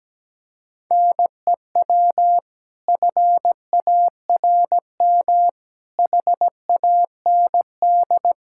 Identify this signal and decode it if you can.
Morse code, heard as a keyed tone, about 17 wpm, 700 Hz. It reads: NEW FARM HAND
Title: Pangram Challenge v2 - 17wpm